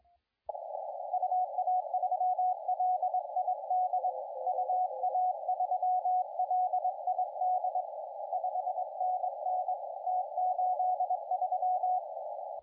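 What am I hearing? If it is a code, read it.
Morse code, heard as a keyed tone, about 22 wpm, 720 Hz, with radio noise. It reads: UA3LID UA3LID 5NN T83
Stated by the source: CW contest audio recorded at LA4C during the SAC CW contest.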